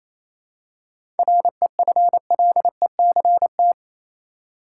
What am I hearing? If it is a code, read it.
Morse code, heard as a keyed tone, about 28 wpm, 700 Hz. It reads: REFLECT
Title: Seven Letter Words - 28wpm